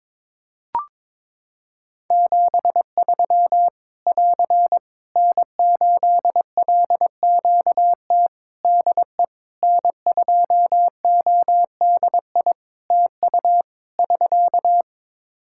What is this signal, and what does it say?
Morse code, heard as a keyed tone, about 22 wpm, 700 Hz. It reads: 73 <AR> N8LQT DE N2ODI TU <SK>
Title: K1USN SST QSO Practice 22wpm